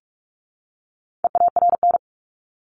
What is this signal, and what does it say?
Morse code, heard as a keyed tone, about 44 wpm, 700 Hz. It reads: EARN
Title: Top 1251-1500 Words 45wpm